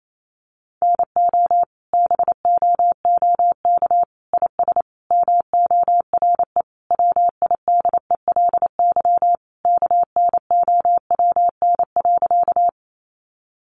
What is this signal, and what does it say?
Morse code, heard as a keyed tone, about 28 wpm, 700 Hz. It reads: NO BOOK IS MORE WIDELY KNOWN.
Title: Sentences from Top 1200 Words 28wpm